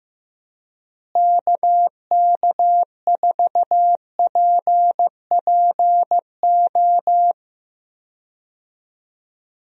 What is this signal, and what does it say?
Morse code, heard as a keyed tone, about 15 wpm, 700 Hz. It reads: KK4PPO
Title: USA Call Signs - 6 Characters - Repeated 2x - 15wpm